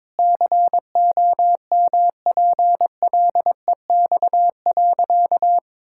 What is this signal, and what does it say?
Morse code, heard as a keyed tone, about 22 wpm, 700 Hz. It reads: COMPLEX.
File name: practice / 22wpm / Sentences with 4-Word Phrases - 22wpm